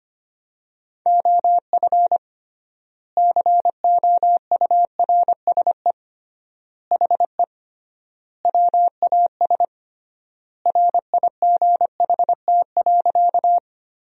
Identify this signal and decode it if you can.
Morse code, heard as a keyed tone, about 25 wpm, 700 Hz. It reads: OF COURSE HE WAS RIGHT.
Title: 3x Word-Spacing - Sentences from Top 500 Words - 25wpm